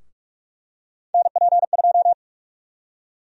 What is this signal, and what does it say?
Morse code, heard as a keyed tone, about 43 wpm, 700 Hz. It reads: NP2
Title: USA Call Sign Prefixes - 45wpm